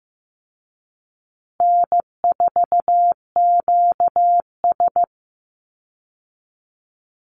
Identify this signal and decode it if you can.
Morse code, heard as a keyed tone, about 15 wpm, 700 Hz. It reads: N4QS
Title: Easy Call Signs Repeated 2x - 15wpm